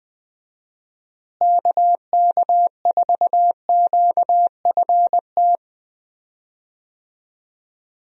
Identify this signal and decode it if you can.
Morse code, heard as a keyed tone, about 20 wpm, 700 Hz. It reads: KK4QFT
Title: USA Call Signs - 6 Characters - Repeated 3x - 20wpm